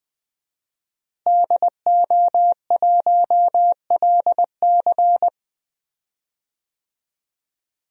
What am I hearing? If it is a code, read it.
Morse code, heard as a keyed tone, about 20 wpm, 700 Hz. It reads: DO1LC
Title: DXCC-100 Call Signs - 5 Characters - Repeated 2x - 20wpm